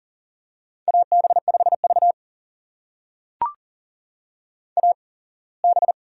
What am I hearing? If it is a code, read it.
Morse code, heard as a keyed tone, about 40 wpm, spaced as for 24 wpm, 700 Hz. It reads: AB5V AB
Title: USA Call Signs - 4 Characters - 8x Character Spacing - 40wpm